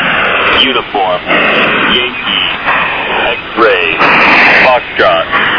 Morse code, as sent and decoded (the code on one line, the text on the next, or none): none